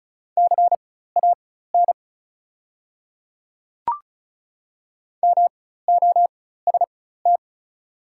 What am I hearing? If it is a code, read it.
Morse code, heard as a keyed tone, about 36 wpm, spaced as for 16 wpm, 700 Hz. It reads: CAN MOST
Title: 4x Character Spacing - Top 200 Words - 35wpm